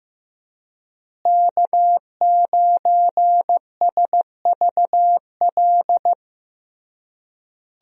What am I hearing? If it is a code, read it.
Morse code, heard as a keyed tone, about 15 wpm, 700 Hz. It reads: K9SVL
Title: USA Call Signs - 5 Characters - Repeated 3x - 15wpm